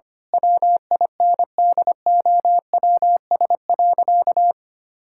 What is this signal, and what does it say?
Morse code, heard as a keyed tone, about 25 wpm, 700 Hz. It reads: WINDOWS.
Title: Pangram Challenge v2 - 25wpm